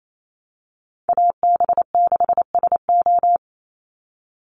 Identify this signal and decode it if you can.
Morse code, heard as a keyed tone, about 28 wpm, 700 Hz. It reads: AB6SO
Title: NA SOTA Chaser Callsigns 28wpm